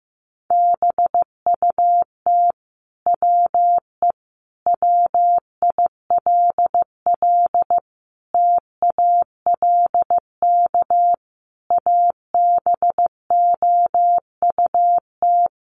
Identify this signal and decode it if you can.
Morse code, heard as a keyed tone, about 15 wpm, 700 Hz. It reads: BUT WE WILL TALK ABOUT
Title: Sentences from Top 1200 Words 15wpm